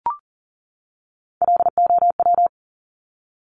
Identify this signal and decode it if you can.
Morse code, heard as a keyed tone, about 40 wpm, 700 Hz. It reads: LOW
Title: Top Three Letter Combinations in Words 40wpm